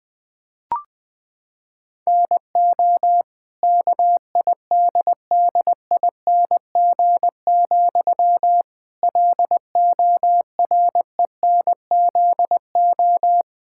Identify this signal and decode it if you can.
Morse code, heard as a keyed tone, about 20 wpm, 700 Hz. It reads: NO KIDDING, LORENZO
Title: Pangram Challenge v2 - 20wpm